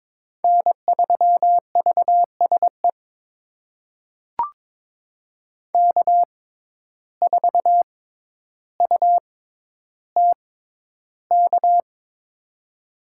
Morse code, heard as a keyed tone, about 23 wpm, spaced as for 11 wpm, 700 Hz. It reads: N3VSE K4UTK
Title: USA Call Signs - 5 Characters - 6x Character Spacing - 22wpm